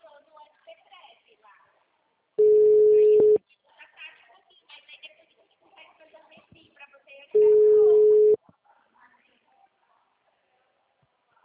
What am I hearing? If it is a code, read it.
Ringback tone (Europe, 425 Hz), with speech.